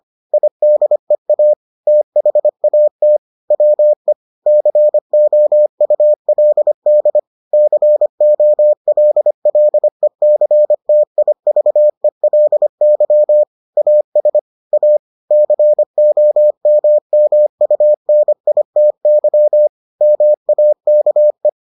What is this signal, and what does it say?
Morse code, heard as a keyed tone, about 25 wpm, 600 Hz. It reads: IDEA THAT WE COULD COLLECTIVELY AS A COMMUNITY MAKE